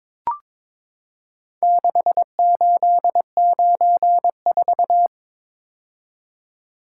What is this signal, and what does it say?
Morse code, heard as a keyed tone, about 23 wpm, 700 Hz. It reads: 6894
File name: practice / 22wpm / Four Letter-Number Combinations - 22wpm